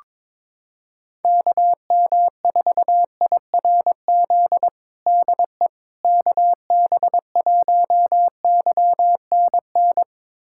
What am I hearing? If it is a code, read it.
Morse code, heard as a keyed tone, about 22 wpm, 700 Hz. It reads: KM4IRZ DE KB1YNN
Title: USA Callsign DE Callsign - 6 Characters - 22wpm